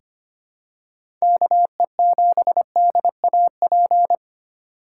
Morse code, heard as a keyed tone, about 25 wpm, 700 Hz. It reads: KE7DAP
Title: USA Call Signs - 4x Character Spacing - 25wpm